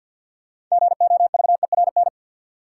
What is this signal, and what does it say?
Morse code, heard as a keyed tone, about 50 wpm, 700 Hz. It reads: GOVERN
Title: Sentences from Top 1000 Words - Review Part 2 - New Word Drill - 50wpm